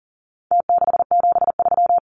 Morse code, heard as a keyed tone, about 40 wpm, 700 Hz. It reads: T673
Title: Numbers - Contest Style - 4-Digit 40wpm